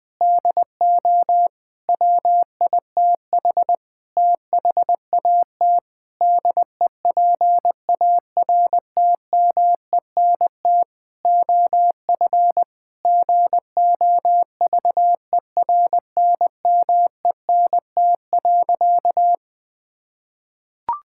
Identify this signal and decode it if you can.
Morse code, heard as a keyed tone, about 20 wpm, 700 Hz. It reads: DO WITH THAT DEPARTMENT OF GOVERNMENT.
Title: Sentences from Top 1000 Words - Review Part 5 - 20wpm